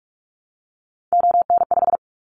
Morse code, heard as a keyed tone, about 46 wpm, 700 Hz. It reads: ON5
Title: Callsign Prefixes for Top 25 DXCC - 45wpm